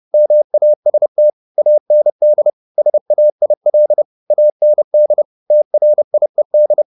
Morse code, heard as a keyed tone, about 30 wpm, 600 Hz. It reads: MAST AND SAIL AND TRIED
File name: the old man and the sea / the old man and the sea 1 30wpm